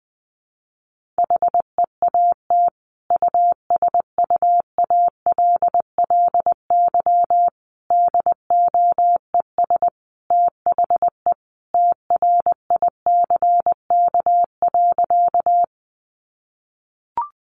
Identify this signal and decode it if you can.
Morse code, heard as a keyed tone, about 20 wpm, 700 Hz. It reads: HEAT USUALLY DOES THE TRICK.
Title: Sentences from Top 1500 Words 20wpm